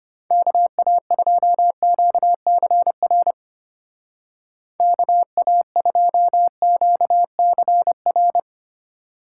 Morse code, heard as a keyed tone, about 31 wpm, 700 Hz. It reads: KA2QCR KA2QCR
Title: Speed Racing - USA 2x3 Call Signs 25wpm